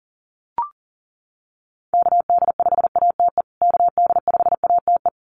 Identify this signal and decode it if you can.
Morse code, heard as a keyed tone, about 40 wpm, 700 Hz. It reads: KD5ATE KD5ATE
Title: USA Call Signs Repeated 2x 40wpm